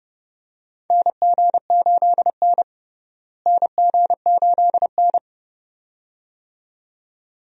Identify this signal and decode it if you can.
Morse code, heard as a keyed tone, about 30 wpm, 700 Hz. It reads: NG8N NG8N
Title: USA Call Signs - 4 Characters - Repeated 2x - 30wpm